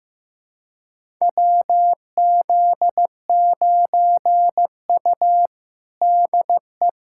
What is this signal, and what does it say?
Morse code, heard as a keyed tone, about 16 wpm, 700 Hz. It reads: WZ9U DE
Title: USA Callsign DE Callsign - 4 Characters - 15wpm